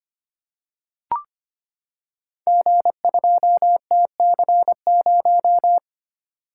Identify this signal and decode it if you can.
Morse code, heard as a keyed tone, about 25 wpm, 700 Hz. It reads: G2TC0
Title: Code Groups - 5 Letter:Numbers - 25wpm